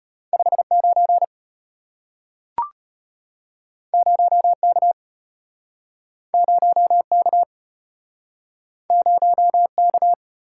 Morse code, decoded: 59 0K 0K 0K

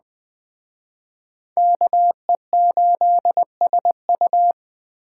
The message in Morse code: KE8SU

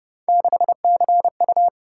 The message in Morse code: 6CU

